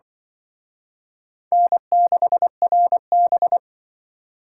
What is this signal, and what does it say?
Morse code, heard as a keyed tone, about 25 wpm, 700 Hz. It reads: N6RB
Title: Speed Racing - USA Call Signs 20wpm